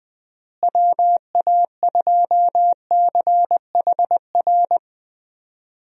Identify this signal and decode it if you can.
Morse code, heard as a keyed tone, about 20 wpm, 700 Hz. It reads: WA2CHR
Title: USA Call Signs - 6 Characters - Repeated 2x - 20wpm